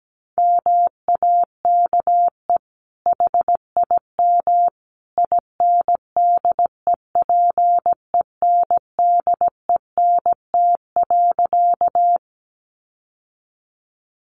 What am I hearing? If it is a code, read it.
Morse code, heard as a keyed tone, about 17 wpm, 700 Hz. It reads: MAKE HIM INDEPENDENT.
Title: Sentences from Top 900 Words 17wpm